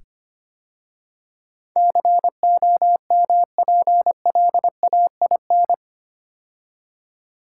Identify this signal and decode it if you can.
Morse code, heard as a keyed tone, about 25 wpm, 700 Hz. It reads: COMPLAIN
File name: 8 Letter Words - First 3 Letters Given - 25wpm